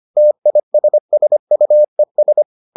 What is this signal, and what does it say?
Morse code, heard as a keyed tone, about 25 wpm, 600 Hz. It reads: TISSUES